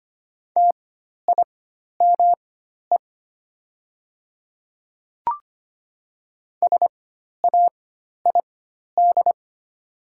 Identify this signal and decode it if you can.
Morse code, heard as a keyed tone, about 25 wpm, spaced as for 12 wpm, 690 Hz. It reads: TIME SAID
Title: 4x Character Spacing - Top 100 Words - 25wpm